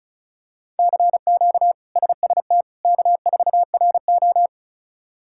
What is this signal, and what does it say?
Morse code, heard as a keyed tone, about 36 wpm, 700 Hz. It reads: CQ SST K4RO
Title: K1USN SST QSO Practice - 35wpm